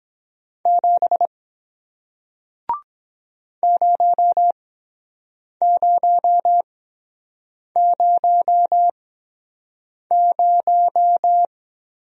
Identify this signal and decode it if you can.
Morse code, heard as a keyed tone, about 23 wpm, spaced as for 7 wpm, 700 Hz. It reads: 7 0000